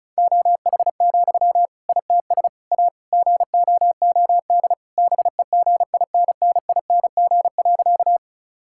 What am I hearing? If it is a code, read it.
Morse code, heard as a keyed tone, about 36 wpm, 700 Hz. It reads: OH, ITS A GOOD BEGINNING.
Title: Sentences from Top 1000 Words - Review Part 5 - 35wpm